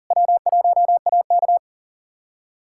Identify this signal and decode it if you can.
Morse code, heard as a keyed tone, about 40 wpm, 700 Hz. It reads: W1AK
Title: USA Callsign DE Callsign - 4 Characters - 40wpm